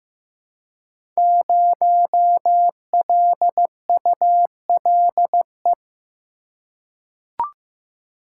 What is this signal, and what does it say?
Morse code, heard as a keyed tone, about 15 wpm, 700 Hz. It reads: 0LULE
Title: Code Groups - 5 Letter:Numbers - 15wpm